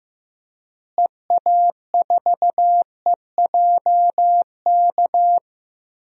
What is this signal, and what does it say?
Morse code, heard as a keyed tone, about 15 wpm, 700 Hz. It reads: EA4EJK
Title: DXCC-100 Call Signs - 6 Characters - Repeated 2x - 15wpm